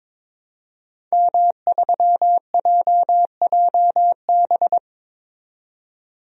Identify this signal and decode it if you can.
Morse code, heard as a keyed tone, about 22 wpm, 700 Hz. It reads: M3JJB
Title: DXCC-100 Call Signs - 5 Characters - 22wpm